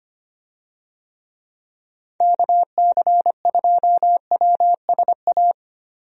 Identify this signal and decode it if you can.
Morse code, heard as a keyed tone, about 25 wpm, 700 Hz. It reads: KC2WSA
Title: USA Call Signs - 6 Characters - 25wpm